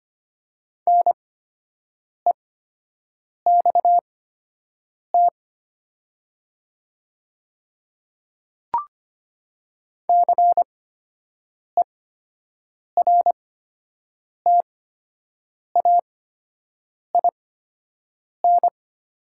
Morse code, heard as a keyed tone, about 25 wpm, spaced as for 7 wpm, 700 Hz. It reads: NEXT CERTAIN
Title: 8x Character Spacing - Top 500 Words - 25wpm